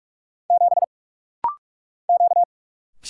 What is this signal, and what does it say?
Morse code, heard as a keyed tone, about 46 wpm, 700 Hz. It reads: 7 Q
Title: Single Letter-Number Alt - Rapid-Fire 45wpm